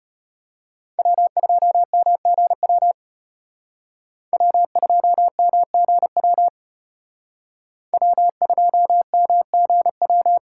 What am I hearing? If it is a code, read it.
Morse code, heard as a keyed tone, about 33 wpm, 700 Hz. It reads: W2MGW W2MGW W2MGW